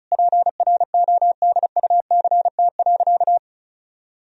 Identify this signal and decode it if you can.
Morse code, heard as a keyed tone, about 36 wpm, 700 Hz. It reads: PRODUCT.
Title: Sentences from Top 1000 Words - Review Part 2 - 35wpm